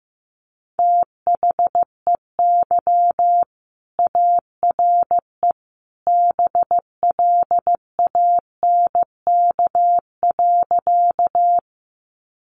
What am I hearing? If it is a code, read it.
Morse code, heard as a keyed tone, about 15 wpm, 700 Hz. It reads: THEY ARE BLANK.
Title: Sentences from Top 1500 Words 15wpm